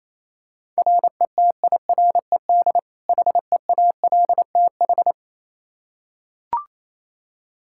Morse code, heard as a keyed tone, about 28 wpm, 700 Hz. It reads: RETIRED HEALTH
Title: Practice QSO - Exchange 3 - More Info - 28wpm